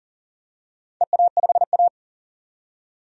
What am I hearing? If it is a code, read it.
Morse code, heard as a keyed tone, about 40 wpm, 700 Hz. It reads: EA5A
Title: DXCC-100 Call Signs - 4 Characters - 40wpm